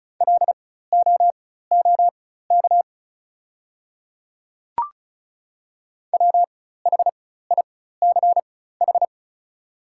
Morse code, heard as a keyed tone, about 36 wpm, spaced as for 16 wpm, 690 Hz. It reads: LOOK WHICH
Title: 4x Character Spacing - Top 100 Words - 35wpm